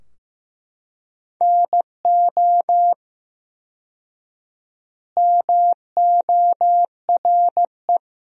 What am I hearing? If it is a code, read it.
Morse code, heard as a keyed tone, about 15 wpm, 700 Hz. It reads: NO MORE
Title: 4x Word-Spacing - Sentences from Top 400 Words - 15wpm